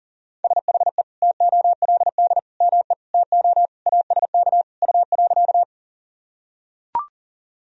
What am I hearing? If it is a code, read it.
Morse code, heard as a keyed tone, about 40 wpm, 700 Hz. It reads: SHE TOLD ME TO ASK U.
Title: Sentences from Top 400 Words and Abbreviations 40wpm